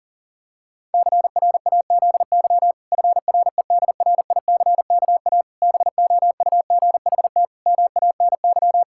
Morse code, heard as a keyed tone, about 40 wpm, 700 Hz. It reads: CRAZY FREDRICKA BOUGHT MANY
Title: Pangram Challenge v2 - 40wpm